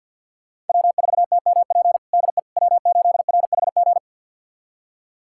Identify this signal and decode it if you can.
Morse code, heard as a keyed tone, about 52 wpm, 700 Hz. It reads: W4TMP DE W8RHG
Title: USA Callsign DE Callsign - 5 Characters - 50wpm